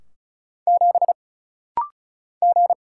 Morse code, heard as a keyed tone, about 34 wpm, 700 Hz. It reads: 7 G
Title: Single Letter-Number Alt - Rapid-Fire 35wpm